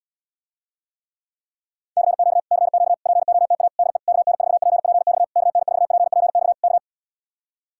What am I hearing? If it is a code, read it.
Morse code, heard as a keyed tone, about 79 wpm, 700 Hz. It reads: CQ CQ POTA DE KN4PPX KN4PPX K